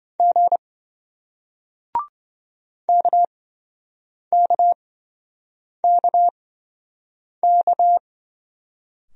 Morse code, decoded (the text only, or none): G KKKK